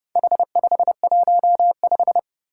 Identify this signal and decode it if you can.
Morse code, heard as a keyed tone, about 30 wpm, 700 Hz. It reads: H515
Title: Four Challenging Letter-Number Combinations 30wpm